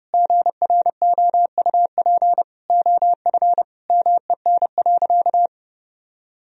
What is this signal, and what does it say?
Morse code, heard as a keyed tone, about 30 wpm, 700 Hz. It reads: GROUP OF MEN.
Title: Sentences from Top 1000 Words - Review Part 1 - 30wpm